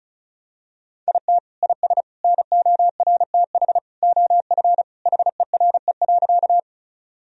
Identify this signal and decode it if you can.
Morse code, heard as a keyed tone, about 36 wpm, 700 Hz. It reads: IT IS NORTH OF HERE.
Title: Sentences from Top 1000 Words - Review Part 1 - 35wpm